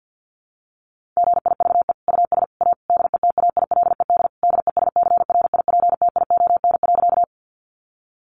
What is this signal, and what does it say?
Morse code, heard as a keyed tone, about 50 wpm, 700 Hz. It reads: GIVE US A DETAILED DESCRIPTION.
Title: Sentences from Top 1500 Words 50wpm